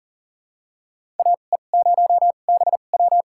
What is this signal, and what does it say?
Morse code, heard as a keyed tone, about 38 wpm, spaced as for 28 wpm, 700 Hz. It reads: AE0BW
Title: USA Call Signs - 2x Character Spacing - 40wpm